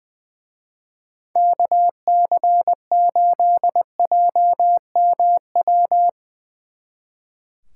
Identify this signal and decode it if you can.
Morse code, heard as a keyed tone, about 21 wpm, 700 Hz. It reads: KC8JMW
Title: Speed Racing - USA Call Signs 20wpm